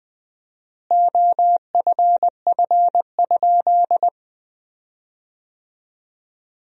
Morse code, heard as a keyed tone, about 20 wpm, 700 Hz. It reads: OFF?
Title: 4x Word-Spacing - Sentences from Top 500 Words - 20wpm